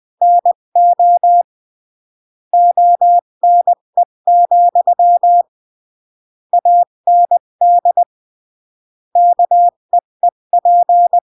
Morse code, heard as a keyed tone, about 20 wpm, 700 Hz. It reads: NO ONE, AND KEEP
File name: CW / jodorowsky / 82 commandments 20-15